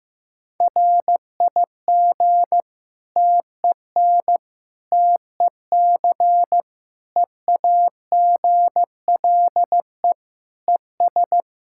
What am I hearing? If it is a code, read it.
Morse code, heard as a keyed tone, about 15 wpm, 700 Hz. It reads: RIG TEN TEC EAGLE ES
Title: Practice QSO - Exchange 2 - Info - 15wpm